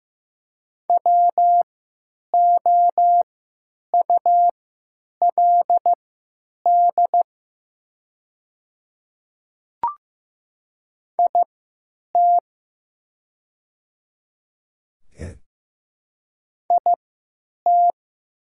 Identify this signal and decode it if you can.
Morse code, heard as a keyed tone, about 15 wpm, spaced as for 8 wpm, 700 Hz. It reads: WOULD IT IT
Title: 3x Character Spacing - Top 100 Words - 15wpm